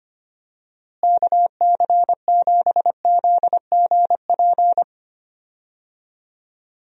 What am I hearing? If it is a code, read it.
Morse code, heard as a keyed tone, about 25 wpm, 700 Hz. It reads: KC7ZGP
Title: USA Call Signs - 6 Characters - Repeated 2x - 25wpm